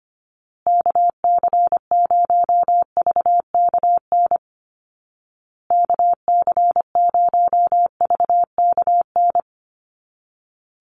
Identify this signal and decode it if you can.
Morse code, heard as a keyed tone, about 25 wpm, 700 Hz. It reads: KC0VKN KC0VKN
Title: Call Signs Repeated 2x - 25wpm